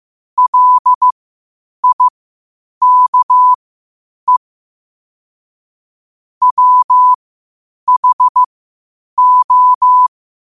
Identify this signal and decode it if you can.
Morse code, heard as a keyed tone, about 15 wpm, spaced as for 8 wpm, 1000 Hz. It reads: LIKE WHO
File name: qotdfiles / 8wpm